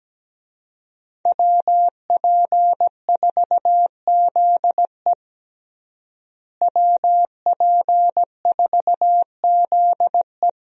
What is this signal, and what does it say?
Morse code, heard as a keyed tone, about 17 wpm, 700 Hz. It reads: WP4ZE WP4ZE
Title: USA Call Signs - 5 Characters - Repeated 3x - 17wpm